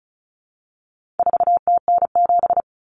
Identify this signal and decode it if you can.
Morse code, heard as a keyed tone, about 34 wpm, 700 Hz. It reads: 4TN7
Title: Common Cut Numbers - 4-Digit 35wpm